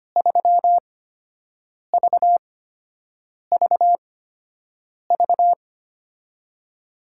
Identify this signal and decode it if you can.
Morse code, heard as a keyed tone, about 25 wpm, spaced as for 7 wpm, 700 Hz. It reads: 3VVV
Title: USA Call Signs - 8x Character Spacing - 25wpm